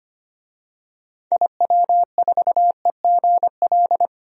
Morse code, heard as a keyed tone, about 25 wpm, 700 Hz. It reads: IW4EGL
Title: DXCC-100 Call Signs - 6 Characters - 25wpm